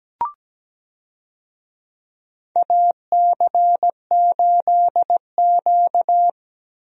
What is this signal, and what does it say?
Morse code, heard as a keyed tone, about 17 wpm, 700 Hz. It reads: AC8Q
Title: USA Call Signs - 4 Characters - 17wpm